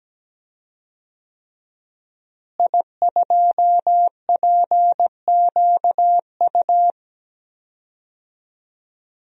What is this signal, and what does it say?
Morse code, heard as a keyed tone, about 17 wpm, 700 Hz. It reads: I2PQU
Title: DXCC-100 Call Signs - 5 Characters - Repeated 3x - 17wpm